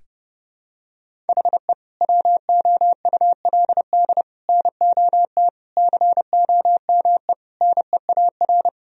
Morse code, heard as a keyed tone, about 31 wpm, 700 Hz. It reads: HE WOULD NOT COME NEAR
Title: Sentences from Top 1000 Words - Review Part 1 - 30wpm